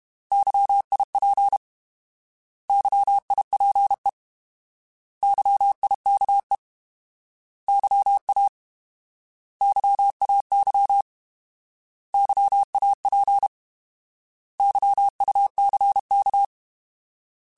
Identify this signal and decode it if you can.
Morse code, heard as a keyed tone, about 32 wpm, 800 Hz. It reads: YIP YIPE YIKE YA YAY YAP YUCK
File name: Alphabetical / Alphabetical mp3s / Ys @30